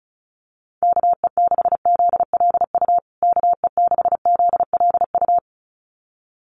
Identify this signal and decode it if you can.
Morse code, heard as a keyed tone, about 34 wpm, 700 Hz. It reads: KE6ZLU KE6ZLU
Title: USA Call Signs Repeated 2x 35wpm